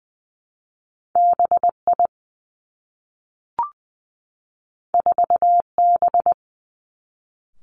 Morse code, heard as a keyed tone, about 20 wpm, 700 Hz. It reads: B I 4 B
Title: Two Challenging Letter-Number Combinations 20wpm